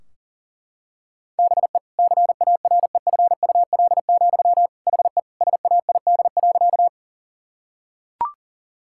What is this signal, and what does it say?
Morse code, heard as a keyed tone, about 40 wpm, 700 Hz. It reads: BE CAREFUL, HE SAID.
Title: Sentences from Top 1600 Words 40wpm